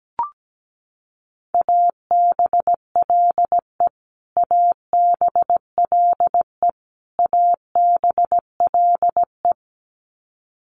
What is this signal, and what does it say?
Morse code, heard as a keyed tone, about 17 wpm, 700 Hz. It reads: ABLE ABLE ABLE
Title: Top 500 Words Repeated 17wpm